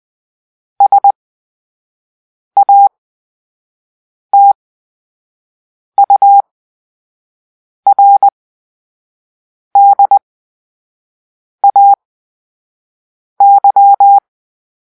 Morse code, heard as a keyed tone, about 20 wpm, spaced as for 5 wpm, 800 Hz. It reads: SATURDAY
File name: Grp-3 Drill-4